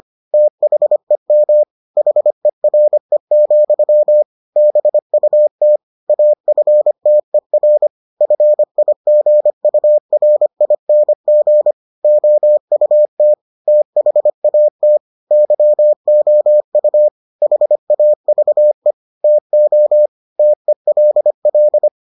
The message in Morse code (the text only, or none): THEM HERE, BUT AFTER FIGURING OUT THAT YOU HAVE TO TELL